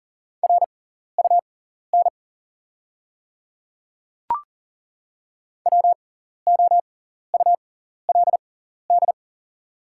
Morse code, heard as a keyed tone, about 40 wpm, spaced as for 14 wpm, 690 Hz. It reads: RUN WOULD
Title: 6x Character Spacing - Top 500 Words - 40wpm